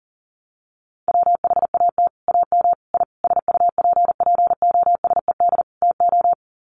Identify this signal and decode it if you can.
Morse code, heard as a keyed tone, about 37 wpm, 700 Hz. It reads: WHAT AM I SUPPOSED TO
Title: Sets of 5 Words Encore2 40wpm